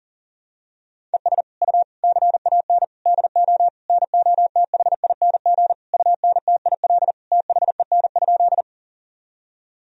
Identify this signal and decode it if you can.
Morse code, heard as a keyed tone, about 37 wpm, 700 Hz. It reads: ES U CAN DO NOTHING UNTIL THEN?
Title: Sentences from Top 500 Words and Abbreviations 40wpm